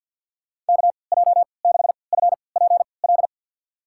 Morse code, heard as a keyed tone, about 47 wpm, spaced as for 28 wpm, 700 Hz. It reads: KJ6FPL
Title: USA Call Signs - 6 Characters - 3x Character Spacing - 50wpm